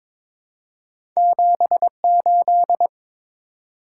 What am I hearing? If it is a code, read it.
Morse code, heard as a keyed tone, about 22 wpm, 700 Hz. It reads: 78
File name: practice / 22wpm / Two Letter-Number Combinations - 22wpm